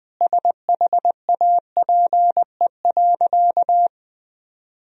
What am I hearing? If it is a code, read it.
Morse code, heard as a keyed tone, about 20 wpm, 700 Hz. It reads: SHAPE.
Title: Sentences from Top 1000 Words - Review Part 3 - 20wpm